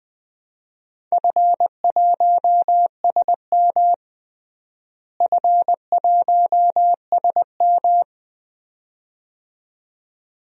Morse code, heard as a keyed tone, about 20 wpm, 700 Hz. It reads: F1SM F1SM
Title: DXCC-100 Call Signs - 4 Characters - Repeated 2x - 20wpm